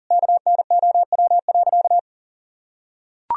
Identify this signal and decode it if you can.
Morse code, heard as a keyed tone, about 40 wpm, 700 Hz. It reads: KNOW.
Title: Sentences from Top 1000 Words - Review Part 3 - 40wpm